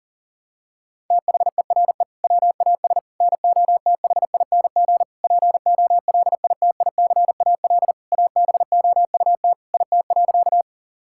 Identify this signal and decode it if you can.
Morse code, heard as a keyed tone, about 40 wpm, 700 Hz. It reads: THERE WAS NOTHING POLITICAL ABOUT IT.
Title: Sentences from Top 1000 Words - Review Part 5 - 40wpm